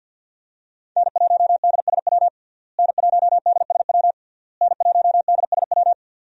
Morse code, heard as a keyed tone, about 46 wpm, 700 Hz. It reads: N1DSW N1DSW N1DSW
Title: USA Call Signs - 5 Characters - Repeated 3x - 50wpm